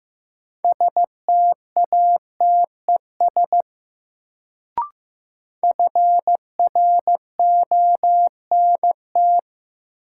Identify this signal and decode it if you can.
Morse code, heard as a keyed tone, about 15 wpm, 700 Hz. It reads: STATES FRONT
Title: Sets of 2 Words - New Word Drill - 15wpm